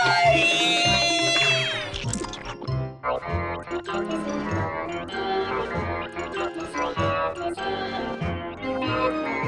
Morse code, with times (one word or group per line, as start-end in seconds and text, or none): none